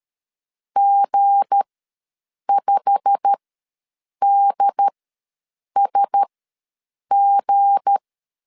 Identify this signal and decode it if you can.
Morse code, heard as a keyed tone, about 13 wpm, spaced as for 7 wpm, 800 Hz. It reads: G5DSG
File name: LEZIONI / 8 YJ45 / 8 YJ45 ws 12wpm 3146